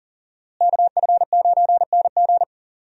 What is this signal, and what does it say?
Morse code, heard as a keyed tone, about 40 wpm, 700 Hz. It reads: KF9NG
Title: USA Call Signs - 5 Characters - Repeated 3x - 40wpm